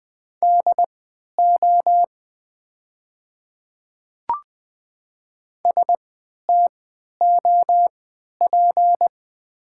Morse code, heard as a keyed tone, about 20 wpm, spaced as for 11 wpm, 700 Hz. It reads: DO STOP
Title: 3x Character Spacing - Top 300 Words - 20wpm